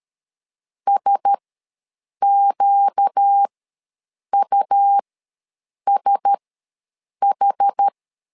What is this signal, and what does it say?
Morse code, heard as a keyed tone, about 13 wpm, spaced as for 7 wpm, 800 Hz. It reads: SQUSH
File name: LEZIONI / 4 Q1S ws 12wpm 1827